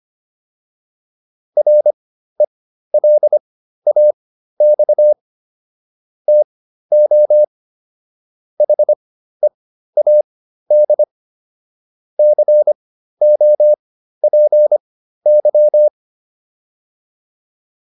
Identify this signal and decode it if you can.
Morse code, heard as a keyed tone, about 25 wpm, spaced as for 13 wpm, 600 Hz. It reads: RELAX TO HEAD COPY